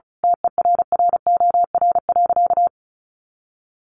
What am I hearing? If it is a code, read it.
Morse code, heard as a keyed tone, about 34 wpm, 700 Hz. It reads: TERROR.
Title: A Sound Of Thunder 35wpm